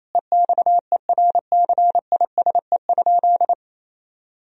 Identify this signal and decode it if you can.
Morse code, heard as a keyed tone, about 28 wpm, 700 Hz. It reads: EXERCISE?
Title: Sentences from Top 1000 Words - Review Part 4 - 28wpm